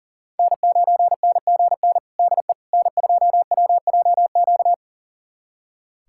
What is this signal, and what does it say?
Morse code, heard as a keyed tone, about 40 wpm, 700 Hz. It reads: N9NGN DE N2WJQ
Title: USA Callsign DE Callsign 40wpm